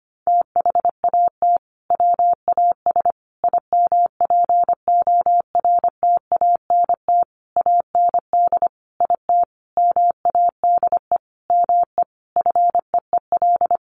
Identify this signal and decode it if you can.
Morse code, heard as a keyed tone, about 25 wpm, 700 Hz. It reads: THAT WAS IMPORTANT AND IT MADE ME FEEL
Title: Sentences from Top 600 Words 25wpm